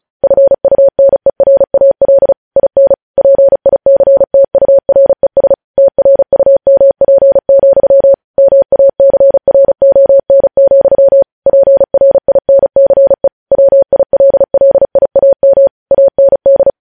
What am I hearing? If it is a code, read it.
Morse code, heard as a keyed tone, about 36 wpm, 550 Hz. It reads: FUNERAL IN PICTURES TRUMP, MACRON, PRINCE WILLIAM AND
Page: News headlines in Morse code at 35 wpm